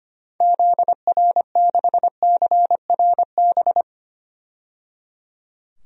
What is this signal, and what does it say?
Morse code, heard as a keyed tone, about 25 wpm, 700 Hz. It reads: ZR6CRB
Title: DXCC-100 Call Signs - 6 Characters - 25wpm